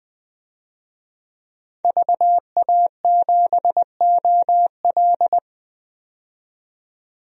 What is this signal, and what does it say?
Morse code, heard as a keyed tone, about 20 wpm, 700 Hz. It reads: VA7OL
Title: DXCC-100 Call Signs - 5 Characters - 20wpm